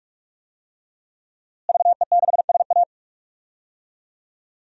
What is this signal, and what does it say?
Morse code, heard as a keyed tone, about 46 wpm, 700 Hz. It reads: VE6SA
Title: DXCC-100 Call Signs - 5 Characters - Repeated 3x - 45wpm